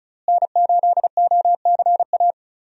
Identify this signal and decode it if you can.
Morse code, heard as a keyed tone, about 36 wpm, 700 Hz. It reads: N8OCA
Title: USA Call Signs - 5 Characters - Repeated 3x - 35wpm